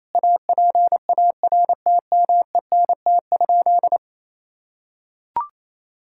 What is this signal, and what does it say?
Morse code, heard as a keyed tone, about 28 wpm, 700 Hz. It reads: APARTMENT?
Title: Sentences from Top 1000 Words - Review Part 6 - 28wpm